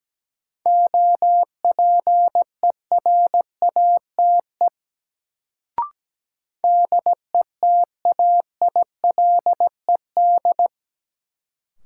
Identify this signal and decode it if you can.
Morse code, heard as a keyed tone, about 17 wpm, 700 Hz. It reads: OPERATE DETAILED
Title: Sentences from Top 1500 Words - New Word Drill Alt - 17wpm